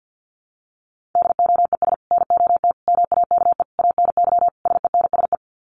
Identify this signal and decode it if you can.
Morse code, heard as a keyed tone, about 50 wpm, 700 Hz. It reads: DOES NOT MAKE ANY SENSE
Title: Sets of 5 Words Encore2 50wpm